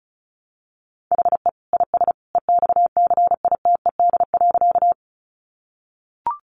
Morse code, heard as a keyed tone, about 34 wpm, 700 Hz. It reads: HE IS EXCITED.
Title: Additional Easy Phrases 35wpm